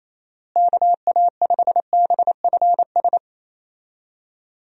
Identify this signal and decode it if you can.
Morse code, heard as a keyed tone, about 28 wpm, 700 Hz. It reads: KA5BFS
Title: USA Call Signs - 6 Characters - Repeated 2x - 28wpm